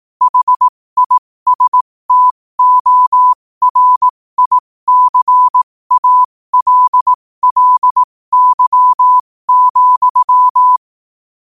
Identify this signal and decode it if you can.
Morse code, heard as a keyed tone, about 18 wpm, spaced as for 15 wpm, 1000 Hz. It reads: HISTORICALLY,
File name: qotdfiles / 15wpm